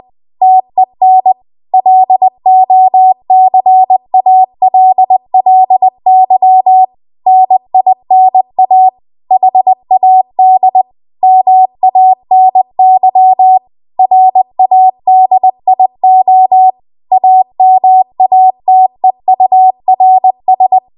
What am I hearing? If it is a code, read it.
Morse code, heard as a keyed tone, about 20 wpm, 750 Hz. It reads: TEN LOCALLY NINA HAD MANY RADIO AMATEURS
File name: Archive / 20 WPM / 20WPM